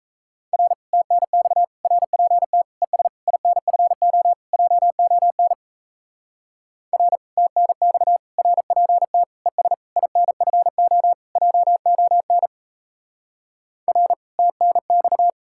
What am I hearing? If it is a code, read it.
Morse code, heard as a keyed tone, about 34 wpm, 700 Hz. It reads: R TNX RPT ES INFO JON R TNX RPT ES INFO JON R TNX
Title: Speed-Racing Common QSO Phrases - 28wpm